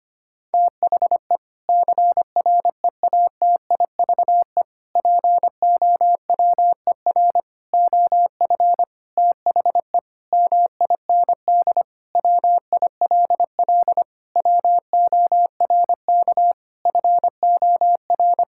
Morse code, heard as a keyed tone, about 25 wpm, 700 Hz. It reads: THE CREATIVE POWER OF THE MIND WILL WORK FOR
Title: Sentences from Top 1000 Words - Review Part 5 - 25wpm